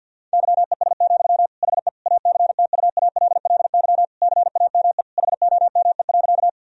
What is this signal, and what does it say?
Morse code, heard as a keyed tone, about 51 wpm, 700 Hz. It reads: YES, HE ACTUALLY CAME HOME.
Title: Sentences from Top 1000 Words - Review Part 4 - 50wpm